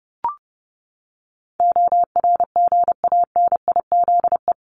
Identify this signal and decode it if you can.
Morse code, heard as a keyed tone, about 30 wpm, 700 Hz. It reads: ORGANIZE
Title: Top 1751-2000 Words 30wpm